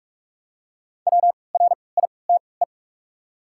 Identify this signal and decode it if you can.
Morse code, heard as a keyed tone, about 44 wpm, spaced as for 25 wpm, 700 Hz. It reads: WRITE
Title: 3x Character Spacing - Top 300 Words - 45wpm